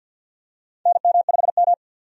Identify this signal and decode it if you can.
Morse code, heard as a keyed tone, about 52 wpm, 700 Hz. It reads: NM5M
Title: USA Call Signs - 4 Characters - 50wpm